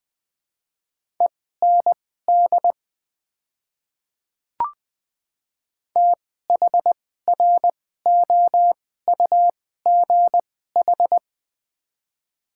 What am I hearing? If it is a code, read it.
Morse code, heard as a keyed tone, about 20 wpm, spaced as for 15 wpm, 700 Hz. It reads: END THROUGH
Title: 2x Character Spacing - Top 200 Words - 20wpm